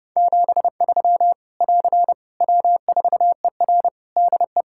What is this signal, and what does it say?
Morse code, heard as a keyed tone, about 30 wpm, 700 Hz. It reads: 73 <AR> W4ER DE
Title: K1USN SST QSO Practice 30wpm